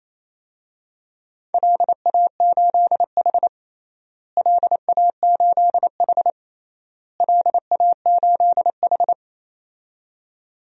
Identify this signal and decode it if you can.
Morse code, heard as a keyed tone, about 27 wpm, 700 Hz. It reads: LA8H LA8H LA8H
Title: DXCC-100 Call Signs - 4 Characters - Repeated 3x - 28wpm